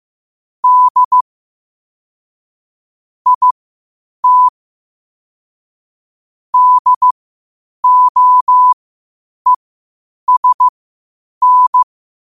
Morse code, heard as a keyed tone, about 15 wpm, spaced as for 8 wpm, 1000 Hz. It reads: D IT DOESN